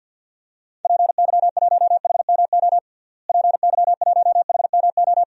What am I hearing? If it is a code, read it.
Morse code, heard as a keyed tone, about 46 wpm, 700 Hz. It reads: PY1HMO PY1HMO
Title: DXCC-100 Call Signs - 6 Characters - Repeated 2x - 50wpm